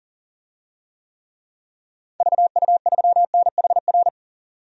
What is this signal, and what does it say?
Morse code, heard as a keyed tone, about 38 wpm, 700 Hz. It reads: VU3NHR
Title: DXCC-100 Call Signs - 6 Characters - 40wpm